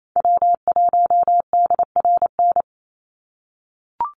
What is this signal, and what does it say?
Morse code, heard as a keyed tone, about 28 wpm, 700 Hz. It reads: W1DRN
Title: USA Call Signs 28wpm